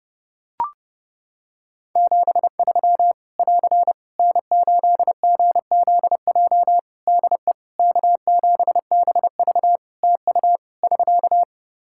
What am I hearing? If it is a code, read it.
Morse code, heard as a keyed tone, about 30 wpm, 700 Hz. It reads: 73 <AR> N8GZJ DE K7BV TU <SK>
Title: Practice QSO - Exchange 4 - Ending - 30wpm